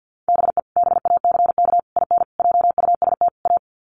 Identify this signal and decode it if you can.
Morse code, heard as a keyed tone, about 50 wpm, 700 Hz. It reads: BE BACK IN JUST A
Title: Sets of 5 Words Encore 50wpm